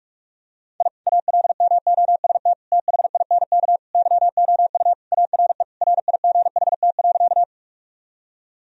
Morse code, heard as a keyed tone, about 41 wpm, 700 Hz. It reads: I ALMOST THINK YOU ARE RIGHT.
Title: Sentences from Top 1000 Words - Review Part 4 - 45wpm